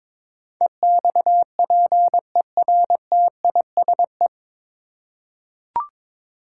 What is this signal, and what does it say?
Morse code, heard as a keyed tone, about 22 wpm, 700 Hz. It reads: EXPERTISE
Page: Morse Code Ninja practice 22wpm